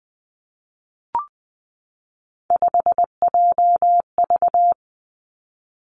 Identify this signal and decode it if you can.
Morse code, heard as a keyed tone, about 20 wpm, 700 Hz. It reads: 5JV
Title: Three Challenging Letter-Number Combinations 20wpm